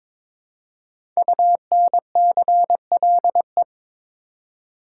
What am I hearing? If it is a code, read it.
Morse code, heard as a keyed tone, about 22 wpm, 700 Hz. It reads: UNCLE
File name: 5 Letter Words - First 3 Letters Given - 22wpm